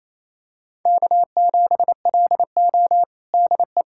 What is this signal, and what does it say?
Morse code, heard as a keyed tone, about 27 wpm, 700 Hz. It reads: K7LO DE
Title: USA Callsign DE Callsign - 4 Characters - 28wpm